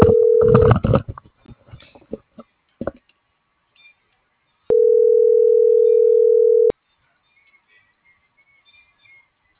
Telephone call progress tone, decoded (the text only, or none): ringback tone